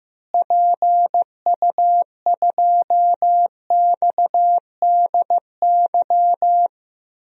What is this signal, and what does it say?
Morse code, heard as a keyed tone, about 15 wpm, 700 Hz. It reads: PU2XDY
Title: DXCC-100 Call Signs - 6 Characters - Repeated 2x - 15wpm